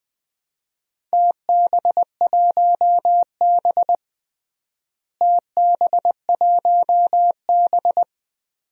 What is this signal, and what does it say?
Morse code, heard as a keyed tone, about 20 wpm, 700 Hz. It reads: TB1B TB1B
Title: DXCC-100 Call Signs - 4 Characters - Repeated 2x - 20wpm